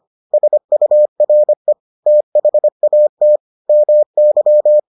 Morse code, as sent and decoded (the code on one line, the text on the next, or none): ... ..- .-. / - .... .- - / -- -.--
SURE THAT MY